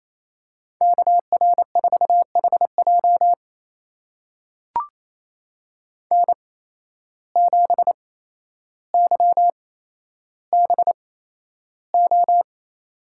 Morse code, heard as a keyed tone, about 28 wpm, spaced as for 12 wpm, 700 Hz. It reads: KR4HJ N7YBO
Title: USA Call Signs - 5 Characters - 8x Character Spacing - 28wpm